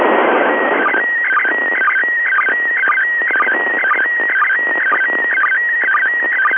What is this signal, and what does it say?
SSTV (slow-scan television), PD 120.